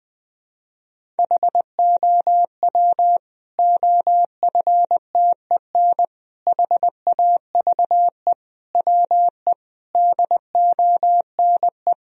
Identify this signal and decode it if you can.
Morse code, heard as a keyed tone, about 19 wpm, 700 Hz. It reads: HOW OFTEN HAVE WE DONE
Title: Sentences from Top 1000 Words - Review Part 1 - 20wpm